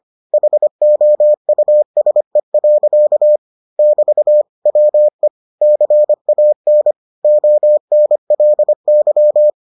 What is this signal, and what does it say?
Morse code, heard as a keyed tone, about 25 wpm, 600 Hz. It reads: HOUSE. <BT> WE CAN ONLY